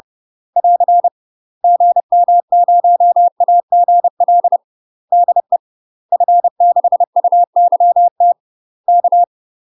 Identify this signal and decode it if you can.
Morse code, heard as a keyed tone, about 30 wpm, 700 Hz. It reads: <AR> GM0AGL DE F6UYT K
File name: UK MASTER QSOs TRAINING / 30WPM / UK MASTER QSOs 0009